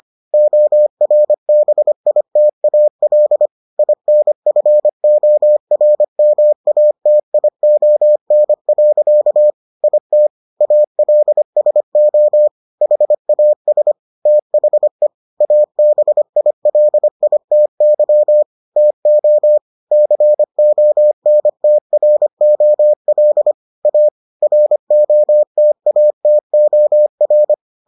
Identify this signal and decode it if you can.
Morse code, heard as a keyed tone, about 25 wpm, 600 Hz. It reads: ORBITAL INFORMATION. IT ALSO HAS THE ABILITY TO CONTROL A ROTATOR